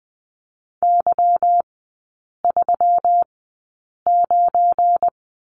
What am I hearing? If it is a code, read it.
Morse code, heard as a keyed tone, about 20 wpm, spaced as for 8 wpm, 700 Hz. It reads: Y39
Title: 191 Lesson 31e - Letter v - Three Characters 20wpm